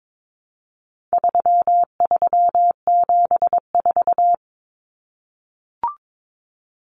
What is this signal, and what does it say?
Morse code, heard as a keyed tone, about 22 wpm, 700 Hz. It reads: 3374
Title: Numbers - 4-Digit 22wpm